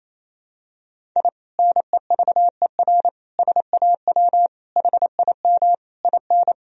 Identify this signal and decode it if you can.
Morse code, heard as a keyed tone, about 28 wpm, 700 Hz. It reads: I NEVER SAW HIM IN
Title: Sentences from Top 1000 Words - Review Part 2 - 28wpm